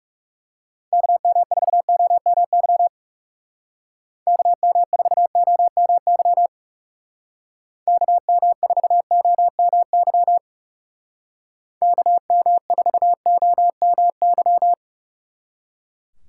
Morse code, heard as a keyed tone, about 39 wpm, 700 Hz. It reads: KM4OMY KM4OMY KM4OMY KM4OMY